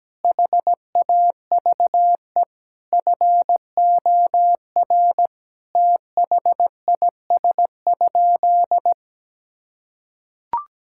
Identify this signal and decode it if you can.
Morse code, heard as a keyed tone, about 17 wpm, 700 Hz. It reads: HAVE FOR THIS?
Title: Sentences from Top 1000 Words - Review Part 5 - 17wpm